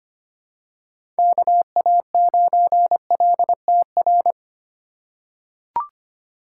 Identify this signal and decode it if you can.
Morse code, heard as a keyed tone, about 25 wpm, 700 Hz. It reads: KA9LTR
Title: USA Call Signs - 6 Characters - 2x Character Spacing - 25wpm